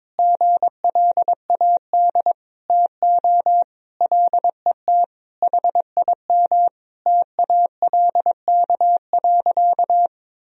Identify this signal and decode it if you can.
Morse code, heard as a keyed tone, about 22 wpm, 700 Hz. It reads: GLAD TO LET HIM TALK.
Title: Sentences from Top 1000 Words - Review Part 4 - 22wpm